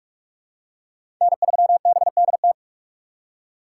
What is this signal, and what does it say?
Morse code, heard as a keyed tone, about 46 wpm, 700 Hz. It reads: N3BDT
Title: USA Call Signs - 5 Characters - 2x Character Spacing - 45wpm